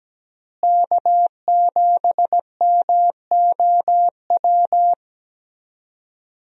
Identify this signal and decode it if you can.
Morse code, heard as a keyed tone, about 17 wpm, 700 Hz. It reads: K7MOW
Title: USA Call Signs - 5 Characters - Repeated 2x - 17wpm